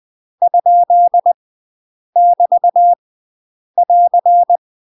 Morse code, heard as a keyed tone, about 20 wpm, 700 Hz. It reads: ? <BT> <AR>
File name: UK MASTER QSOs TRAINING / 20WPM / UK MASTER QSOs 0046